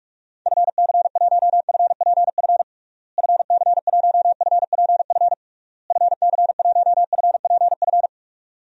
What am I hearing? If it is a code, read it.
Morse code, heard as a keyed tone, about 46 wpm, 700 Hz. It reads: FC1FPF FC1FPF FC1FPF
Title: DXCC-100 Call Signs - 6 Characters - Repeated 3x - 45wpm